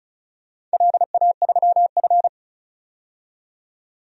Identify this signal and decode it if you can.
Morse code, heard as a keyed tone, about 34 wpm, 700 Hz. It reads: LA3F
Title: DXCC-100 Call Signs - 4 Characters - 35wpm